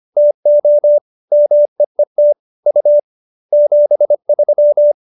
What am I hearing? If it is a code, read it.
Morse code, heard as a keyed tone, about 25 wpm, 590 Hz. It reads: TO MEET U 73